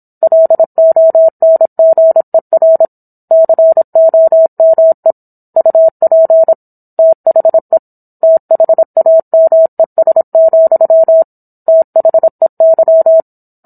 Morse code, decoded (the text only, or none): LONGER COME UP THE THAMES, THEY